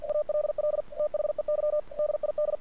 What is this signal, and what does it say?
Morse code, heard as a keyed tone, about 47 wpm, 600 Hz. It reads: ALL THEY DID
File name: archived / CRSnet